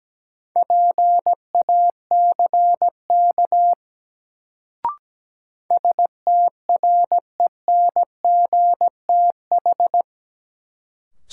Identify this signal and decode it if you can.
Morse code, heard as a keyed tone, about 17 wpm, 700 Hz. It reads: PACK STRENGTH
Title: Sentences from Top 1300 Words - New Word Drill Alt - 17wpm